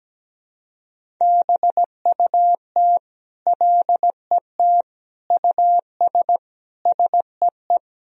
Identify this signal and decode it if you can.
Morse code, heard as a keyed tone, about 17 wpm, 700 Hz. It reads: BUT LET US SEE
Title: Sentences from Top 500 Words and Abbreviations 17wpm